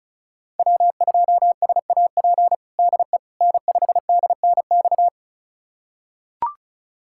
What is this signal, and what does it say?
Morse code, heard as a keyed tone, about 36 wpm, 700 Hz. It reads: W2SAP DE N5DNX
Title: USA Callsign DE Callsign - 5 Characters - 35wpm